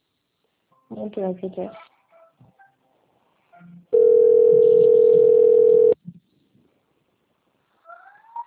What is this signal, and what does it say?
Ringback tone (North America), 440 + 480 Hz, with speech.
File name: out- - - - - .79598